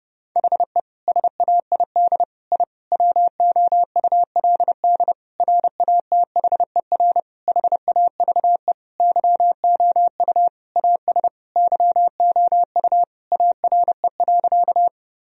Morse code, heard as a keyed tone, about 30 wpm, 700 Hz. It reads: HE SAID I WOULD RATHER HAVE YOU AS YOU ARE.
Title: Sentences from Top 1000 Words - Review Part 4 - 30wpm